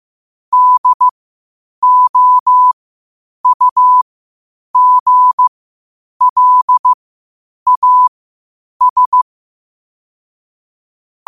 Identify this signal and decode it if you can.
Morse code, heard as a keyed tone, about 15 wpm, spaced as for 8 wpm, 1000 Hz. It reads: DOUGLAS
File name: qotdfiles / 8wpm